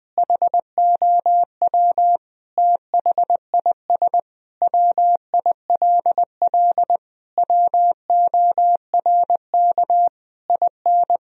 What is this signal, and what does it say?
Morse code, heard as a keyed tone, about 21 wpm, 700 Hz. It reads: HOW THIS WILL WORK IN
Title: Sentences from Top 1000 Words - Review Part 4 - 20wpm